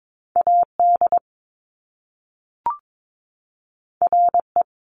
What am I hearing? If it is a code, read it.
Morse code, heard as a keyed tone, about 22 wpm, 700 Hz. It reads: A D R E
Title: Top Two Letter Combinations in Words 22wpm